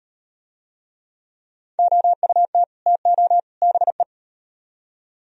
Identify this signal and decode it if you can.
Morse code, heard as a keyed tone, about 34 wpm, 700 Hz. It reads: OUT TO BE